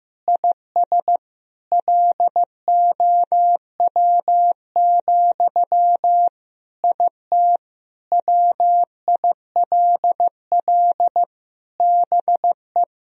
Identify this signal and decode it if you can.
Morse code, heard as a keyed tone, about 16 wpm, 700 Hz. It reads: IS LOW, IT WILL BE
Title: Sentences from Top 1000 Words - Review Part 1 - 15wpm